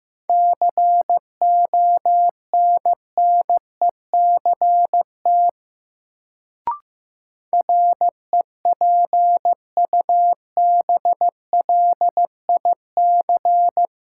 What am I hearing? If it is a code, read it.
Morse code, heard as a keyed tone, about 15 wpm, 700 Hz. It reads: CONNECT REPUBLIC
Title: Sentences from Top 1500 Words - New Word Drill - 15wpm